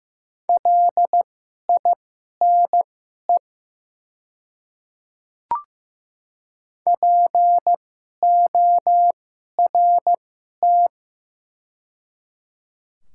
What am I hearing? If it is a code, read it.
Morse code, heard as a keyed tone, about 15 wpm, spaced as for 11 wpm, 700 Hz. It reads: LINE PORT
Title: 2x Character Spacing - Top 200 Words - 15wpm